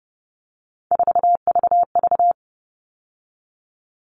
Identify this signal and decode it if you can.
Morse code, heard as a keyed tone, about 30 wpm, 700 Hz. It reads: 4VV
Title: All Cut Numbers - 3-Digit 30wpm